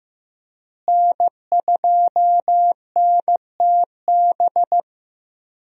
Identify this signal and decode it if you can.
Morse code, heard as a keyed tone, about 14 wpm, 700 Hz. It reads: N2NTB
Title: USA Call Signs - 5 Characters - Repeated 3x - 15wpm